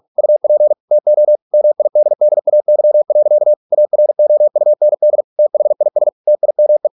Morse code, heard as a keyed tone, about 46 wpm, 600 Hz. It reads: UP TO MIDDAY. AROUND THIS TIME